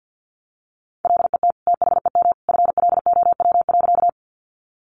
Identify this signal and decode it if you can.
Morse code, heard as a keyed tone, about 50 wpm, 700 Hz. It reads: LET THEM FLOW.
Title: Sentences from Top 1200 Words 50wpm